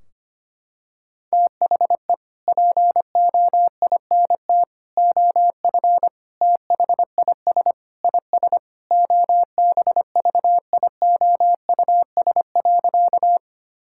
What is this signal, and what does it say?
Morse code, heard as a keyed tone, about 25 wpm, 700 Hz. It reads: THE POINT OF THIS IS OBVIOUS.
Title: Sentences from Top 1600 Words 25wpm